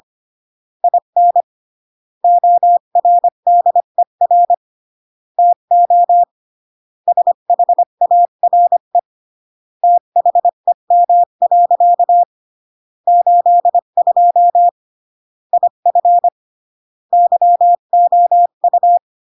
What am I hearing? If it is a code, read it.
Morse code, heard as a keyed tone, about 25 wpm, 700 Hz. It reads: IN ORDER TO SHARE THEM. 82 IF YOU